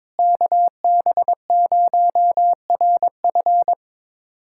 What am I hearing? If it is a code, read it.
Morse code, heard as a keyed tone, about 22 wpm, 700 Hz. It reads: KB0RF
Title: USA Callsign DE Callsign - 5 Characters - 22wpm